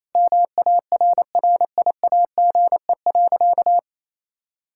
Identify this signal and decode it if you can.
Morse code, heard as a keyed tone, about 28 wpm, 700 Hz. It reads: MARRIAGE.
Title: Sentences from Top 1000 Words - Review Part 6 - 28wpm